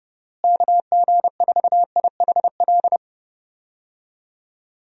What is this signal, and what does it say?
Morse code, heard as a keyed tone, about 32 wpm, 700 Hz. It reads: KG4IHL
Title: USA Call Signs - 6 Characters - Repeated 2x - 30wpm